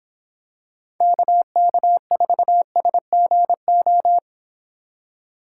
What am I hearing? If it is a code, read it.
Morse code, heard as a keyed tone, about 26 wpm, 700 Hz. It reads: KK4SGO